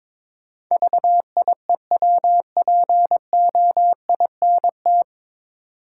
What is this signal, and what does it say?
Morse code, heard as a keyed tone, about 22 wpm, 700 Hz. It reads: VIEWPOINT
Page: Morse Code Ninja practice 22wpm